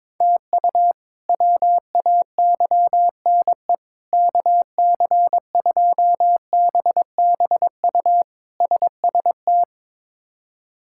Morse code, heard as a keyed tone, about 22 wpm, 700 Hz. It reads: TU WAYNE KC2BBU SST
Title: K1USN SST QSO Practice - 22wpm